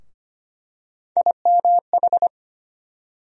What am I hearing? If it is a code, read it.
Morse code, heard as a keyed tone, about 25 wpm, 700 Hz. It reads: IMH
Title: USA Call Sign Postfixes - 3 Letters 25wpm